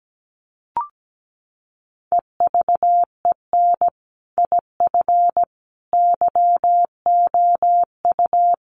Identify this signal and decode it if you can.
Morse code, heard as a keyed tone, about 17 wpm, 700 Hz. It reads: EVEN IF YOU
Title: Sets of 5 Words Encore 17wpm